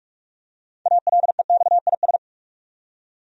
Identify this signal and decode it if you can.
Morse code, heard as a keyed tone, about 46 wpm, 700 Hz. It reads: ALEXIS
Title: USA Common Names - 45wpm